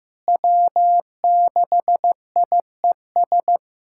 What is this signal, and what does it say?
Morse code, heard as a keyed tone, about 15 wpm, 700 Hz. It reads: W6IES
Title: USA Callsign DE Callsign - 5 Characters - 15wpm